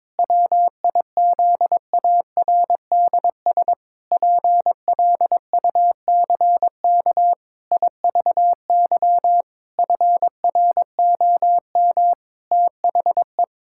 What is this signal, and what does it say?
Morse code, heard as a keyed tone, about 22 wpm, 700 Hz. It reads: WIZARDS PLUCK IVY FROM THE
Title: Pangram Challenge v2 - 22wpm